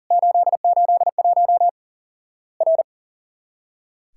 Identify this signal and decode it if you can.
Morse code, heard as a keyed tone, about 40 wpm, 700 Hz. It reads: 881 R
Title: Worked All Europe DX Contest - QTC Exchange - 40wpm